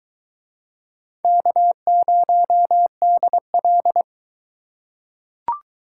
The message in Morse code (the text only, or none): K0DL